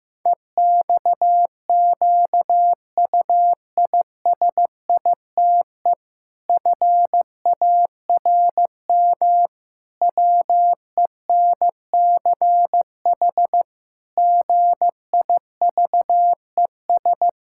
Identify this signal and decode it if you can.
Morse code, heard as a keyed tone, about 16 wpm, 700 Hz. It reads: EXQUISITE FARM WENCH GIVES
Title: Pangram Challenge v2 - 15wpm